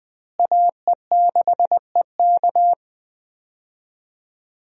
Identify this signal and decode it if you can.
Morse code, heard as a keyed tone, about 20 wpm, 700 Hz. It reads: AE6EK
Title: USA Call Signs - 5 Characters - Repeated 3x - 20wpm